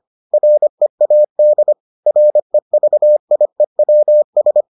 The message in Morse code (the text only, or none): READ REVIEWS